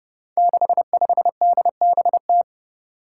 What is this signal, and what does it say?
Morse code, heard as a keyed tone, about 31 wpm, 700 Hz. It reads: 65DBT
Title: Five Challenging Letter-Number Combinations 30wpm